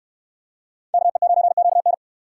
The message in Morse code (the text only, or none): KE9QN